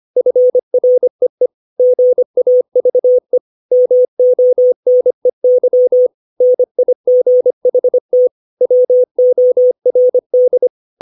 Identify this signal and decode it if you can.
Morse code, heard as a keyed tone, about 25 wpm, 500 Hz. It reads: FREE GAVE MONEY NIGHT WORD
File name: 632 words / 632 Words 25WPM